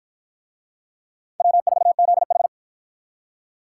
Morse code, heard as a keyed tone, about 48 wpm, 700 Hz. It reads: W4ZH